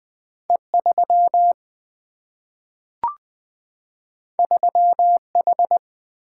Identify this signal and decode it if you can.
Morse code, heard as a keyed tone, about 20 wpm, 700 Hz. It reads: E3 3H
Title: Single-Letter Single-Number 20wpm